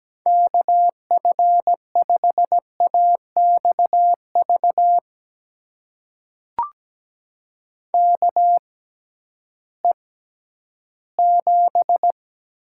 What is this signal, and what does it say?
Morse code, heard as a keyed tone, about 17 wpm, spaced as for 11 wpm, 700 Hz. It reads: KF5AXV KE7
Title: USA Call Signs - 6 Characters - 6x Character Spacing - 17wpm